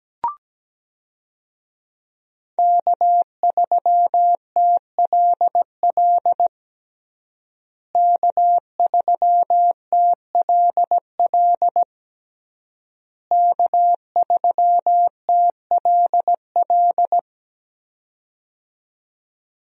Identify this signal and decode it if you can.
Morse code, heard as a keyed tone, about 17 wpm, 700 Hz. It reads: K3TLL K3TLL K3TLL
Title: USA Call Signs - 5 Characters - Repeated 3x - 17wpm